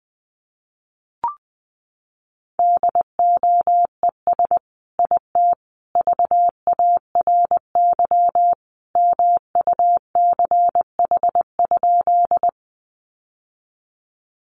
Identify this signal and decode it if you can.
Morse code, heard as a keyed tone, about 19 wpm, 700 Hz. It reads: DOES IT VARY MUCH?
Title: Sentences from Top 1200 Words 20wpm